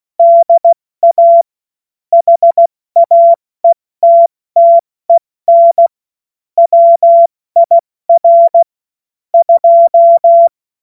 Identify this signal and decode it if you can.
Morse code, heard as a keyed tone, about 16 wpm, 680 Hz. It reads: DA HAETTEN WIR 2
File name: Morsekurs / Witze / Witz 07.November 2023 70 F80